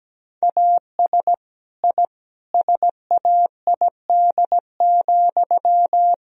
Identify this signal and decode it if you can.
Morse code, heard as a keyed tone, about 17 wpm, 700 Hz. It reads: AS I SAID,
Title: Sentences from Top 1000 Words - Review Part 3 - 17wpm